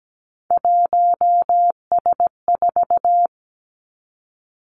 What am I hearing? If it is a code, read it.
Morse code, heard as a keyed tone, about 17 wpm, 700 Hz. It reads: 1S4
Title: Three Challenging Letter-Number Combinations 17wpm